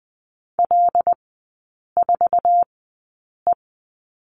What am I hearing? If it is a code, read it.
Morse code, heard as a keyed tone, about 20 wpm, spaced as for 8 wpm, 700 Hz. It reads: L4E
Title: 121 Lesson 20e - Number 3 - Three Characters 20wpm